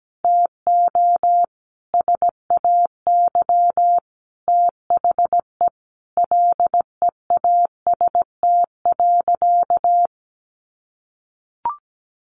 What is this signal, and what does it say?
Morse code, heard as a keyed tone, about 17 wpm, 700 Hz. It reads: TO SAY THE LEAST.
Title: Sentences from Top 1200 Words 17wpm